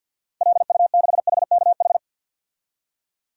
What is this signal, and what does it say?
Morse code, heard as a keyed tone, about 49 wpm, 700 Hz. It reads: LU6HKH
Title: DXCC-100 Call Signs - 6 Characters - Repeated 3x - 50wpm